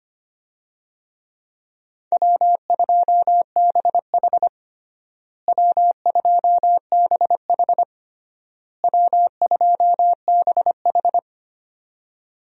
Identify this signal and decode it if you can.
Morse code, heard as a keyed tone, about 25 wpm, 700 Hz. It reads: W2BH W2BH W2BH
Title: USA Call Signs - 4 Characters - Repeated 3x - 25wpm